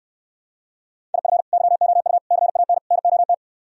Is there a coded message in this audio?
Yes: Morse code from 1.1 to 3.3 s, heard as a keyed tone, about 68 wpm, 700 Hz. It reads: IF YOU CAN NOT